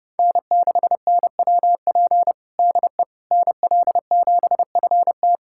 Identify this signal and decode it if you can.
Morse code, heard as a keyed tone, about 30 wpm, 700 Hz. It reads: N6NWP DE NL7FT
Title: USA Callsign DE Callsign - 5 Characters - 30wpm